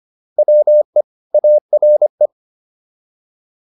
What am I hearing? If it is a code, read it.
Morse code, heard as a keyed tone, about 25 wpm, 600 Hz. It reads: WE ARE
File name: WD104 25